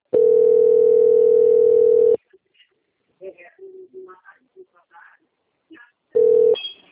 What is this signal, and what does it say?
Ringback tone (North America), 440 + 480 Hz, with speech.
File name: out- - - - - .26873